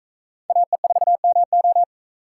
Morse code, heard as a keyed tone, about 40 wpm, 700 Hz. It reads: AE4MO